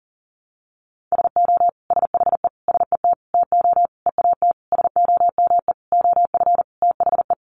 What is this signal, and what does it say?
Morse code, heard as a keyed tone, about 40 wpm, 700 Hz. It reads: SO SHE SET TO EAT SOME OF THE
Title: Sentences from Top 900 Words 40wpm